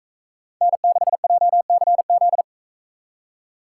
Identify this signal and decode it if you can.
Morse code, heard as a keyed tone, about 40 wpm, 700 Hz. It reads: N6JCZ